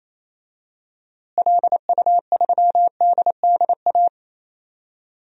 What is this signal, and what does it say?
Morse code, heard as a keyed tone, about 28 wpm, 700 Hz. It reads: LU3DDA
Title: DXCC-100 Call Signs - 6 Characters - Repeated 2x - 28wpm